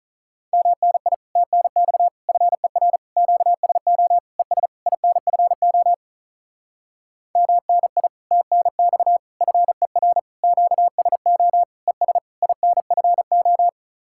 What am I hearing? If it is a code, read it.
Morse code, heard as a keyed tone, about 43 wpm, 700 Hz. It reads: MNI TNX FER QSO ES INFO MNI TNX FER QSO ES INFO
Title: Speed-Racing Common QSO Phrases - 35wpm